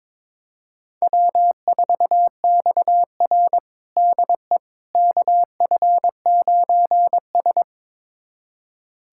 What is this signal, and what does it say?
Morse code, heard as a keyed tone, about 22 wpm, 700 Hz. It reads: W4XR DE KF9S
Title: USA Callsign DE Callsign - 4 Characters - 22wpm